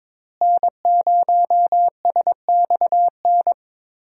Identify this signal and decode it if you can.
Morse code, heard as a keyed tone, about 22 wpm, 700 Hz. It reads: N0SXN
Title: USA Call Signs - 5 Characters - Repeated 2x - 22wpm